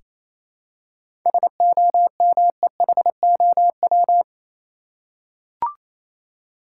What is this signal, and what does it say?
Morse code, heard as a keyed tone, about 28 wpm, 700 Hz. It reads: SOMEHOW
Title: Seven Letter Words - 28wpm